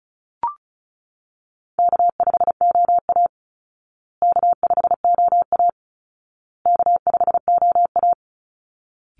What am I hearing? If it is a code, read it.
Morse code, heard as a keyed tone, about 34 wpm, 700 Hz. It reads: K5OA K5OA K5OA
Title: Easy Call Signs Repeated 3x - 35wpm